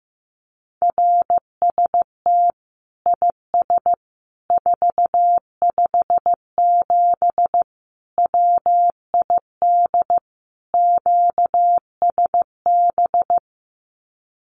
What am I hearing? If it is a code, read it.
Morse code, heard as a keyed tone, about 15 wpm, 700 Hz. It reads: RST IS 457 WID QSB
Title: Common QSO Phrases 15wpm